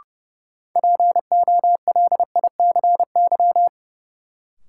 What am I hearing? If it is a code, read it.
Morse code, heard as a keyed tone, about 32 wpm, 700 Hz. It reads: POLICY
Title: Sentences from Top 1000 Words - Review Part 5 - New Word Drill - 30wpm